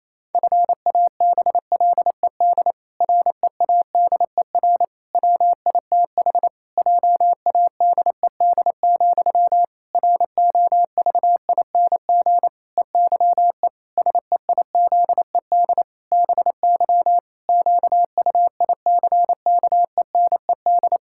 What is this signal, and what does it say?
Morse code, heard as a keyed tone, about 28 wpm, 700 Hz. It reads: FABLED READER WITH JADED, ROVING EYE SEIZED BY QUICKENED
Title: Pangram Challenge v2 - 28wpm